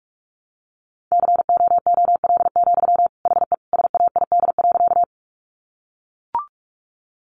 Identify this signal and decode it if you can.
Morse code, heard as a keyed tone, about 44 wpm, 700 Hz. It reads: COOL, HE SAID.
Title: Sentences from Top 1200 Words 45wpm